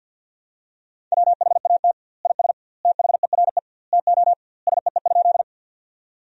Morse code, heard as a keyed tone, about 50 wpm, 700 Hz. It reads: WHAT IS THERE TO SEE?
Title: 2x Word-Spacing - Sentences from Top 100 Words - 50wpm